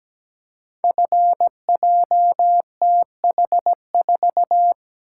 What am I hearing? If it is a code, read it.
Morse code, heard as a keyed tone, about 17 wpm, 700 Hz. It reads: FJTH4
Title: Code Groups - 5 Letter:Numbers - 17wpm